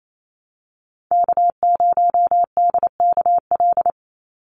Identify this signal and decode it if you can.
Morse code, heard as a keyed tone, about 28 wpm, 700 Hz. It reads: K0DKL
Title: USA Call Signs 28wpm